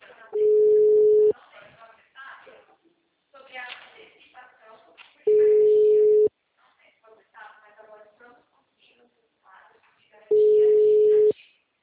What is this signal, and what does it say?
Ringback tone (Europe, 425 Hz), with speech.
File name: out- -2001- - - .5569